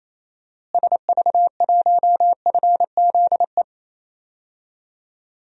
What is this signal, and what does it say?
Morse code, heard as a keyed tone, about 28 wpm, 700 Hz. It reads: SV1FZE
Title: DXCC-100 Call Signs - 6 Characters - Repeated 3x - 28wpm